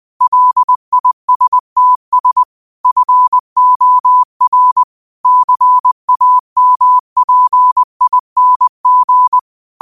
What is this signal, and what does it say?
Morse code, heard as a keyed tone, about 20 wpm, 1000 Hz. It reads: LISTS FOR CAMPING
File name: qotdfiles / 20wpm